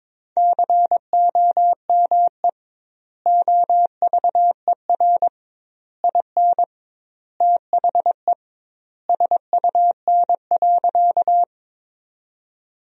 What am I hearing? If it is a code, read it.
Morse code, heard as a keyed tone, about 21 wpm, 700 Hz. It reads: COME OVER IN THE SUN.
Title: 2x Word-Spacing - Sentences from Top 300 Words - 22wpm